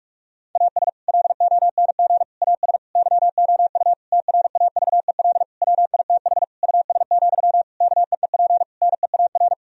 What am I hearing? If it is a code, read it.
Morse code, heard as a keyed tone, about 46 wpm, 700 Hz. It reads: AS LONG AS YOU TRAVEL WITH US, KEEP NEAR
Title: Sentences from Top 1000 Words - Review Part 2 - 45wpm